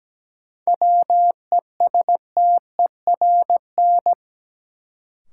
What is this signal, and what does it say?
Morse code, heard as a keyed tone, about 17 wpm, 700 Hz. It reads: WESTERN
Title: Sentences from Top 1000 Words - Review Part 7 - New Word Drill - 17wpm